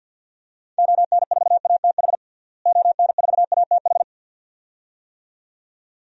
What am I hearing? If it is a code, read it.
Morse code, heard as a keyed tone, about 49 wpm, 700 Hz. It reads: ON4ATH ON4ATH
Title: DXCC-100 Call Signs - 6 Characters - Repeated 2x - 50wpm